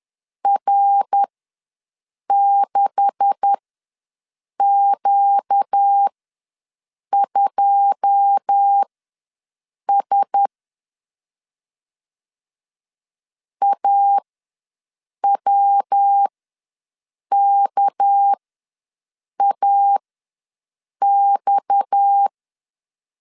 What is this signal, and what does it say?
Morse code, heard as a keyed tone, about 10 wpm, spaced as for 6 wpm, 800 Hz. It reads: R6Q2S AWKAX
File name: LEZIONI / 9 286 ws 10wpm 4032